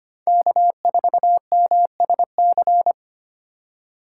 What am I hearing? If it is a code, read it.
Morse code, heard as a keyed tone, about 25 wpm, 700 Hz. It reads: K4MSC
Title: USA Call Signs - 5 Characters - 6x Character Spacing - 25wpm